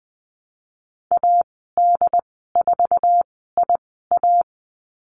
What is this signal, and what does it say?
Morse code, heard as a keyed tone, about 20 wpm, spaced as for 15 wpm, 700 Hz. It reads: AD4IA
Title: 069 Lesson 12g - Letter d - Call Signs 20wpm